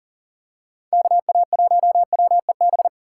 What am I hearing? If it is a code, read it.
Morse code, heard as a keyed tone, about 40 wpm, 700 Hz. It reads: KA1WEB
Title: USA Call Signs - 6 Characters - 3x Character Spacing - 40wpm